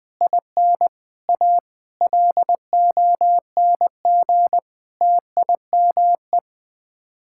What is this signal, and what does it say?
Morse code, heard as a keyed tone, about 20 wpm, 700 Hz. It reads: IN A LONG TIME
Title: Sets of 8 Words - 20wpm